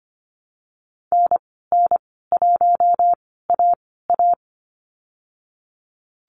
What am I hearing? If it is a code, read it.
Morse code, heard as a keyed tone, about 25 wpm, spaced as for 16 wpm, 700 Hz. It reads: NN1AA
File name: Lesson 15g - Number 5 - Call Signs 25wpm